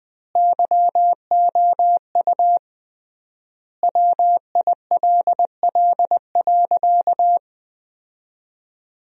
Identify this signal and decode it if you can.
Morse code, heard as a keyed tone, about 20 wpm, 700 Hz. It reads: YOU WILL.
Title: 3x Word-Spacing - Sentences from Top 200 Words - 20wpm